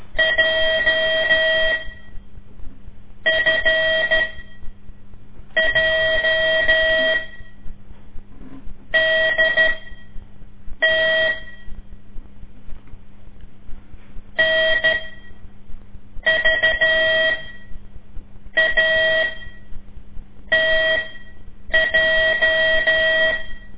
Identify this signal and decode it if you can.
Morse code, heard as a keyed tone, about 13 wpm, spaced as for 5 wpm, 640 Hz. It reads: JFJDT NVATJ
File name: ik1weg2 / CORSI CW / lez5-2